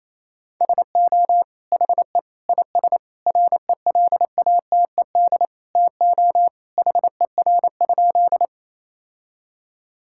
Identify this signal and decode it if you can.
Morse code, heard as a keyed tone, about 28 wpm, 700 Hz. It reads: SO HE IS RELATED TO HER?
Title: Sentences from Top 1000 Words - Review Part 5 - 28wpm